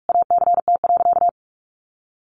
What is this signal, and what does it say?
Morse code, heard as a keyed tone, about 44 wpm, 700 Hz. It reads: ACT.
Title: Sentences from Top 1400 Words 45wpm